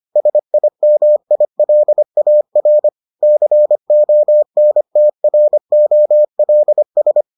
Morse code, heard as a keyed tone, about 25 wpm, 600 Hz. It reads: SIMILAR CONTROLS